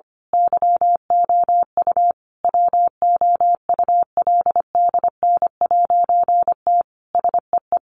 Morse code, heard as a keyed tone, about 25 wpm, 700 Hz. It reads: YOU WOULDN'T SEE
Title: A Sound Of Thunder 25wpm